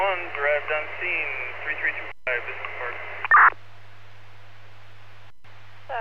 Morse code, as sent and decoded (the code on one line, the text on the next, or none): none